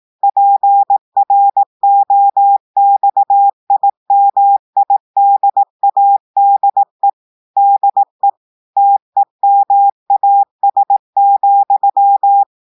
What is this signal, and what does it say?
Morse code, heard as a keyed tone, about 18 wpm, 800 Hz. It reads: PROXIMIDADE DE TEMAS,
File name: audios / portugais / portugais texte en clair 1 vitesse 18 mots minutes